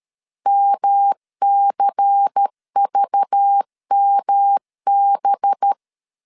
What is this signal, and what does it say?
Morse code, heard as a keyed tone, about 13 wpm, 800 Hz. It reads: MCVMB
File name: LEZIONI / 6 PKX0 / 6 PKX 12wpm 2024